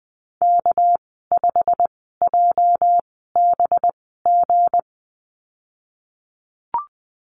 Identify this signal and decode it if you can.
Morse code, heard as a keyed tone, about 20 wpm, spaced as for 15 wpm, 700 Hz. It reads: K5JBG
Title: 219 Lesson 35g - Number 0 - Call Signs 20wpm